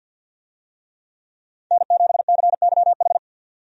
Morse code, heard as a keyed tone, about 53 wpm, 700 Hz. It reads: N7CYH
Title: USA Call Signs - 5 Characters - 50wpm